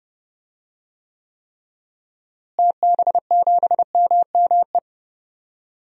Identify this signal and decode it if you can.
Morse code, heard as a keyed tone, about 31 wpm, 700 Hz. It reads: TB7MME
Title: DXCC-100 Call Signs - 6 Characters - Repeated 3x - 30wpm